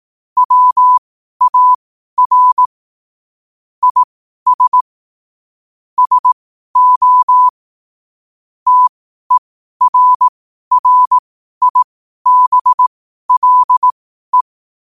Morse code, heard as a keyed tone, about 18 wpm, spaced as for 12 wpm, 1000 Hz. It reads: WAR IS SO TERRIBLE
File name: qotdfiles / 12wpm